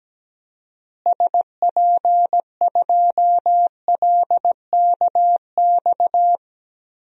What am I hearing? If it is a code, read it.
Morse code, heard as a keyed tone, about 17 wpm, 700 Hz. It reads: SP2LKX
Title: DXCC-100 Call Signs - 6 Characters - Repeated 3x - 17wpm